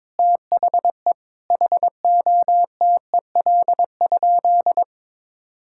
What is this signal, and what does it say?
Morse code, heard as a keyed tone, about 22 wpm, 700 Hz. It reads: THE HOTEL?
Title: Sentences from Top 1600 Words 22wpm